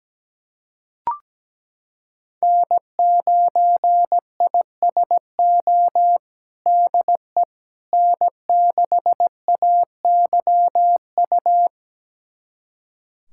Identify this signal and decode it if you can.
Morse code, heard as a keyed tone, about 17 wpm, 700 Hz. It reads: N9ISO DE N6AYU
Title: USA Callsign DE Callsign - 5 Characters - 17wpm